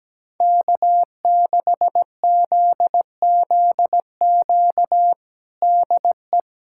Morse code, heard as a keyed tone, about 17 wpm, 700 Hz. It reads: K6ZZQ DE
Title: USA Callsign DE Callsign - 5 Characters - 17wpm